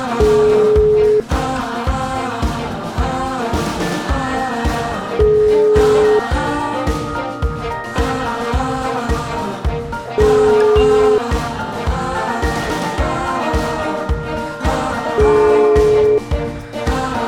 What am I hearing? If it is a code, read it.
Ringback tone (Europe, 425 Hz), with noise.